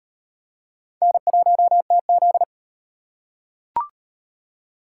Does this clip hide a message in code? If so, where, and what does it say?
Yes: Morse code from 1.0 to 2.4 s, heard as a keyed tone, about 38 wpm, 700 Hz. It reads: N1TZ